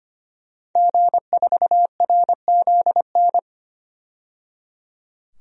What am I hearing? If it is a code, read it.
Morse code, heard as a keyed tone, about 25 wpm, 700 Hz. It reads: G4RZN
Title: DXCC-100 Call Signs - 5 Characters - 25wpm